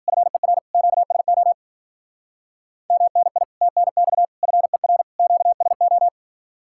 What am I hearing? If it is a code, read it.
Morse code, heard as a keyed tone, about 53 wpm, 700 Hz. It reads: FER QSO MNI TNX FER QSO